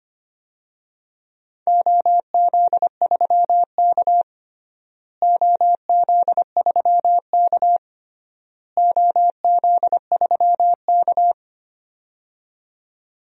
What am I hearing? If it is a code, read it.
Morse code, heard as a keyed tone, about 25 wpm, 700 Hz. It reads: OZ3K OZ3K OZ3K
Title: DXCC-100 Call Signs - 4 Characters - Repeated 3x - 25wpm